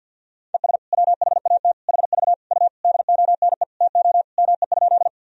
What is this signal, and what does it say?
Morse code, heard as a keyed tone, about 45 wpm, 700 Hz. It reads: ES WHAT HV U DONE TO ME?
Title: Sentences from Top 500 Words and Abbreviations 50wpm